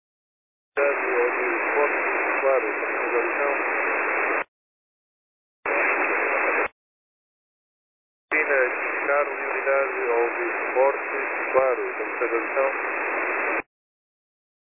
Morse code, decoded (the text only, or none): none